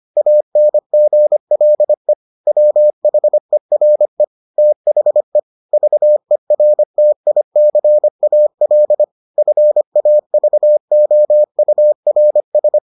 Morse code, heard as a keyed tone, about 25 wpm, 600 Hz. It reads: ANGLE WHERE THE VERTICAL FAVOURS